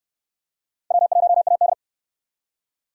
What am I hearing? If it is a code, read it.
Morse code, heard as a keyed tone, about 66 wpm, 700 Hz. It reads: W1AD